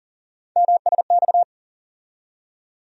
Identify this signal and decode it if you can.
Morse code, heard as a keyed tone, about 40 wpm, 700 Hz. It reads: MSX
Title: USA Call Sign Postfixes - 3 Letters 40wpm